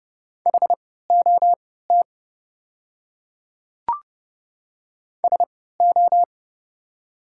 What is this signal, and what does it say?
Morse code, heard as a keyed tone, about 30 wpm, spaced as for 17 wpm, 700 Hz. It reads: HOT SO
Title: 3x Character Spacing - Top 100 Words - 30wpm